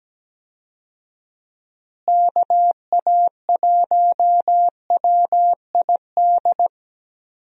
Morse code, heard as a keyed tone, about 17 wpm, 700 Hz. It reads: KA1WID
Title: USA Call Signs - 6 Characters - Repeated 3x - 17wpm